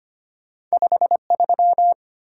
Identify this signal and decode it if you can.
Morse code, heard as a keyed tone, about 25 wpm, 700 Hz. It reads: 53
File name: Two Letter-Number Combinations - 25wpm